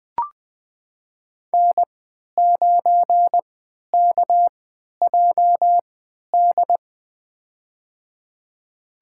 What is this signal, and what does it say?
Morse code, heard as a keyed tone, about 20 wpm, spaced as for 11 wpm, 700 Hz. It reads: N9KJD
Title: USA Call Signs - 3x Character Spacing - 20wpm